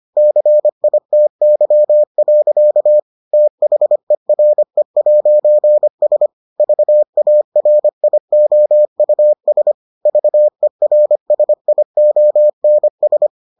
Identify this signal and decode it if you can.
Morse code, heard as a keyed tone, about 25 wpm, 600 Hz. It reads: CITY. THERE'S VARIOUS VERSIONS